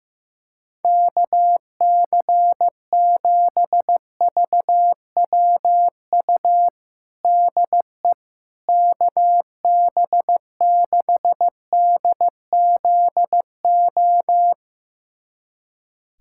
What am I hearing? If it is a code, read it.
Morse code, heard as a keyed tone, about 15 wpm, 700 Hz. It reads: KC7VWU DE KB6DZO
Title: USA Callsign DE Callsign 15wpm